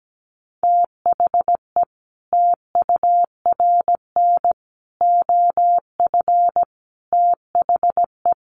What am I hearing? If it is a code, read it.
Morse code, heard as a keyed tone, about 17 wpm, 700 Hz. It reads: THE TURN OF THE
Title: Sets of 5 Words 17wpm